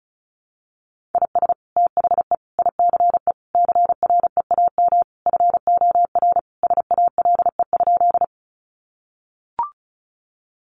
Morse code, heard as a keyed tone, about 34 wpm, 700 Hz. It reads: IS THE ICE CREAM FOR SALE?
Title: Sentences from Top 1500 Words 35wpm